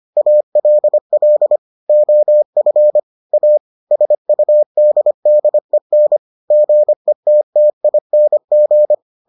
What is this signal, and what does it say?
Morse code, heard as a keyed tone, about 25 wpm, 600 Hz. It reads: ALL OF A SUDDEN GETTING